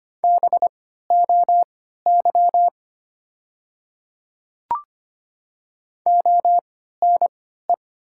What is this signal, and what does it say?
Morse code, heard as a keyed tone, about 25 wpm, spaced as for 14 wpm, 700 Hz. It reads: BOY ONE
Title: 3x Character Spacing - Top 200 Words - 25wpm